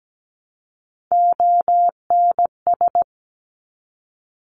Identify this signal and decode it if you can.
Morse code, heard as a keyed tone, about 17 wpm, 700 Hz. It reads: ONS
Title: Top Three Letter Combinations in Words 17wpm